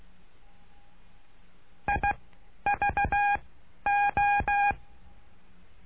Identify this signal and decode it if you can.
Morse code, heard as a keyed tone, about 16 wpm, 810 Hz. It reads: I V O
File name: cursos / CURSO DE TELEGRAFIA / Telegrafia - Trilha 09